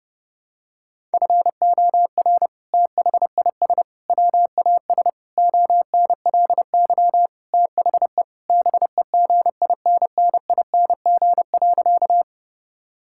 Morse code, heard as a keyed tone, about 30 wpm, 700 Hz. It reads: FOR THIS WAS ONLY THE BEGINNING.
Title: Sentences from Top 1000 Words - Review Part 5 - 30wpm